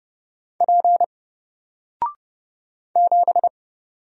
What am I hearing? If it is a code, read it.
Morse code, heard as a keyed tone, about 29 wpm, 700 Hz. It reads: P 7
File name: Single-Letter-Number-30wpm